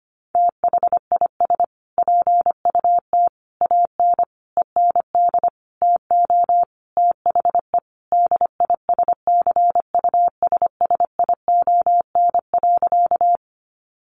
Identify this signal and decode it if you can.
Morse code, heard as a keyed tone, about 25 wpm, 700 Hz. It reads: THIS PUT AN END TO THE DISCUSSION.
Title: Sentences from Top 1100 Words 25wpm